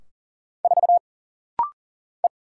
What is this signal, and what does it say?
Morse code, heard as a keyed tone, about 40 wpm, 700 Hz. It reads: 4 E
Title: Single Letter-Number Alt - Rapid-Fire 40wpm